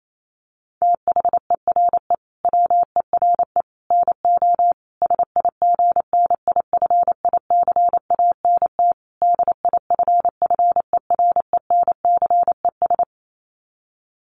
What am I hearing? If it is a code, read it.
Morse code, heard as a keyed tone, about 28 wpm, 700 Hz. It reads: THERE WERE NO SIGNIFICANT DIFFERENCES
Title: Sets of 5 Words Encore2 28wpm